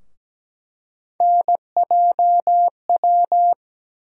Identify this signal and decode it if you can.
Morse code, heard as a keyed tone, about 17 wpm, 700 Hz. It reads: NJW
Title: USA Call Sign Postfixes - 3 Letters 17wpm